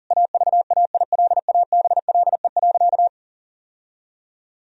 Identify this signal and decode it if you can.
Morse code, heard as a keyed tone, about 40 wpm, 700 Hz. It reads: AVAILABLE.
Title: Sentences from Top 1000 Words - Review Part 3 - 40wpm